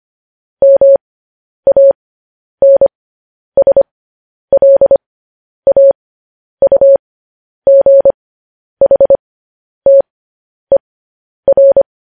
Morse code, heard as a keyed tone, about 25 wpm, spaced as for 10 wpm, 550 Hz. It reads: MANSLAUGHTER
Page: News Headlines in Morse Code at 10 WPM